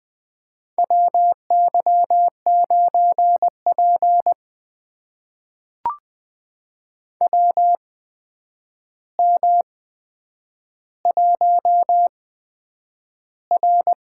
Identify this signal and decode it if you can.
Morse code, heard as a keyed tone, about 20 wpm, spaced as for 9 wpm, 700 Hz. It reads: WY9P WM1R
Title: USA Call Signs - 4 Characters - 8x Character Spacing - 20wpm